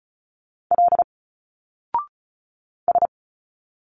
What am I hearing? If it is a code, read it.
Morse code, heard as a keyed tone, about 35 wpm, 700 Hz. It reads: L S
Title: Single Letters 35wpm